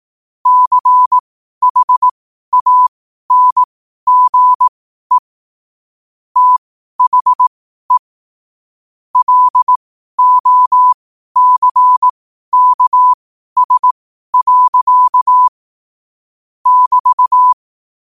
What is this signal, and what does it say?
Morse code, heard as a keyed tone, about 18 wpm, spaced as for 12 wpm, 1000 Hz. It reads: CHANGE THE LOCKS. <BT>
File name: qotdfiles / 12wpm